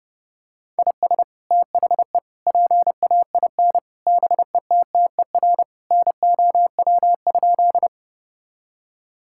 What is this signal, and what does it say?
Morse code, heard as a keyed tone, about 30 wpm, 700 Hz. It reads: IS THE PAIN BETTER NOW?
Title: Sentences from Top 1600 Words 30wpm